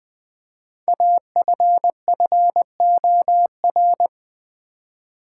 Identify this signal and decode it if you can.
Morse code, heard as a keyed tone, about 20 wpm, 700 Hz. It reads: AFFOR
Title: Build A Word - 5 Letters At A Time - 10-Letter Words - 20wpm